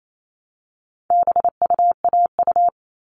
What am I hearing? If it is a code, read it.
Morse code, heard as a keyed tone, about 28 wpm, 700 Hz. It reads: BUAU
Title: All Cut Numbers - 4-Digit 28wpm